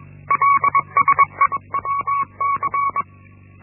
Morse code, heard as a keyed tone, about 21 wpm, 1120 Hz. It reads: LSEWC
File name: Linksmakalnis-1